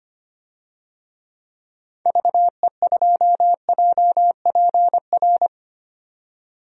Morse code, heard as a keyed tone, about 25 wpm, 700 Hz. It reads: VE2JPR
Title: DXCC-100 Call Signs - 6 Characters - 25wpm